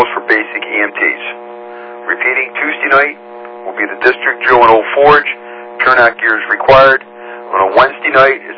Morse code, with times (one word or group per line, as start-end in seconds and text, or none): none